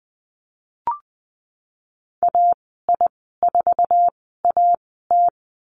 Morse code, heard as a keyed tone, about 20 wpm, spaced as for 15 wpm, 700 Hz. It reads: AI4AT
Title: 055 Lesson 10g - Letter r - Call Signs 20wpm